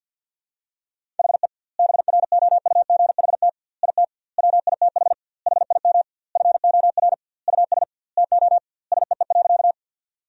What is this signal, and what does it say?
Morse code, heard as a keyed tone, about 50 wpm, 700 Hz. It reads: HE BROUGHT IT WITH HIM FOR US TO SEE.
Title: 2x Word-Spacing - Sentences from Top 500 Words - 50wpm